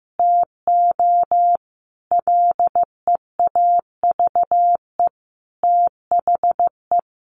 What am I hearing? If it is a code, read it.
Morse code, heard as a keyed tone, about 14 wpm, 700 Hz. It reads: TO LEAVE THE
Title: Easy Phrases 15wpm